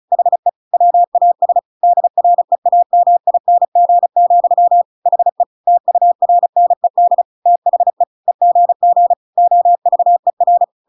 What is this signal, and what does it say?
Morse code, heard as a keyed tone, about 35 wpm, 700 Hz. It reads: HE WAS DREAMING, HE TURNED THE EGG OVER